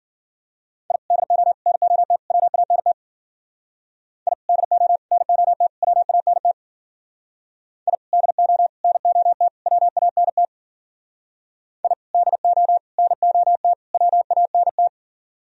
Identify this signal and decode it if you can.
Morse code, heard as a keyed tone, about 52 wpm, 700 Hz. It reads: I DO NOT WANT I DO NOT WANT I DO NOT WANT I DO NOT WANT